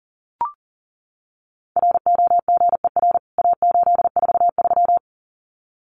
Morse code, heard as a keyed tone, about 40 wpm, 700 Hz. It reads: ROGER A843
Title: CWT Exchange 40wpm